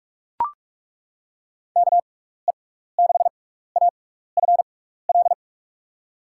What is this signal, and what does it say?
Morse code, heard as a keyed tone, about 42 wpm, spaced as for 15 wpm, 700 Hz. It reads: KE6AFL
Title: USA Call Signs - 6x Character Spacing - 45wpm